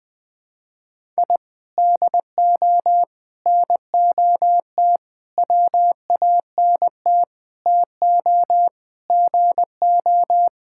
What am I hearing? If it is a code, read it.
Morse code, heard as a keyed tone, about 20 wpm, 700 Hz. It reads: I DO NOT WANT TO GO
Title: Sentences with 5-Word Phrases - 20wpm